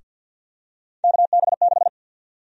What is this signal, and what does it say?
Morse code, heard as a keyed tone, about 51 wpm, 700 Hz. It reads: KB6
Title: USA Call Sign Prefixes - Repeated 2x - 50wpm